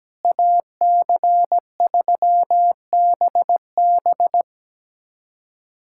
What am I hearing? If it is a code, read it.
Morse code, heard as a keyed tone, about 17 wpm, 700 Hz. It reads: AC3BB
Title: USA Callsign DE Callsign - 5 Characters - 17wpm